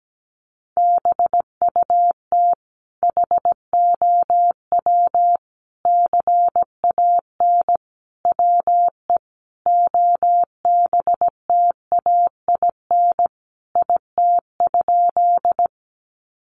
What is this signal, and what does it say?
Morse code, heard as a keyed tone, about 17 wpm, 700 Hz. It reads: BUT HOW CAN WE OBTAIN IT?
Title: Sentences from Top 1300 Words 17wpm